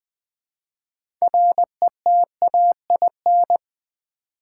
Morse code, heard as a keyed tone, about 20 wpm, 700 Hz. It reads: RETAIN
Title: Build A Word - 3 Letters At A Time - 6-Letter Words - 20wpm